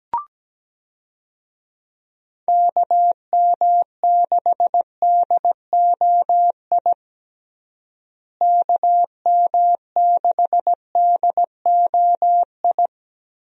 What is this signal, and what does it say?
Morse code, heard as a keyed tone, about 17 wpm, 700 Hz. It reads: KM6DOI KM6DOI
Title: USA Call Signs - 6 Characters - Repeated 3x - 17wpm